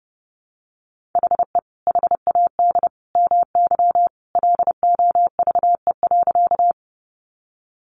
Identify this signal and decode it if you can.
Morse code, heard as a keyed tone, about 30 wpm, 700 Hz. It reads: HE HAD MY LOVE.
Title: Sentences from Top 400 Words 30wpm